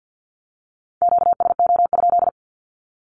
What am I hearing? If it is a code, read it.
Morse code, heard as a keyed tone, about 50 wpm, 700 Hz. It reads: QSO?
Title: Q-Codes with Spoken Meaning - 50wpm